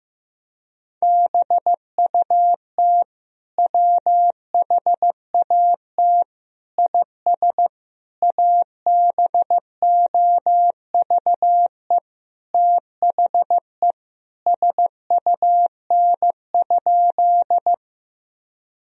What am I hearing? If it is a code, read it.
Morse code, heard as a keyed tone, about 16 wpm, 700 Hz. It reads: BUT WHAT IS ABOVE THE SUN?
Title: Sentences from Top 1000 Words - Review Part 2 - 15wpm